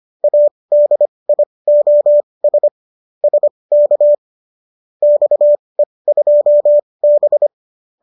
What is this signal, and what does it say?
Morse code, heard as a keyed tone, about 25 wpm, spaced as for 20 wpm, 590 Hz. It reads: ADIOS SK XE2B